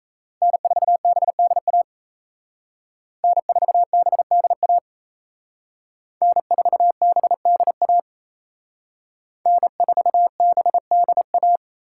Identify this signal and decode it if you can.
Morse code, heard as a keyed tone, about 35 wpm, 700 Hz. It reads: N4BDA N4BDA N4BDA N4BDA